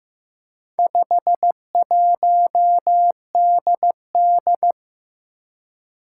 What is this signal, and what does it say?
Morse code, heard as a keyed tone, about 15 wpm, 700 Hz. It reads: 51DD
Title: Four Challenging Letter-Number Combinations 15wpm